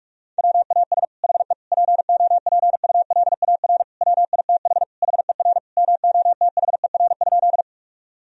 Morse code, heard as a keyed tone, about 46 wpm, 700 Hz. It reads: WAS HE POPULAR WITH HER MOTHER?
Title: Sentences from Top 1000 Words - Review Part 3 - 45wpm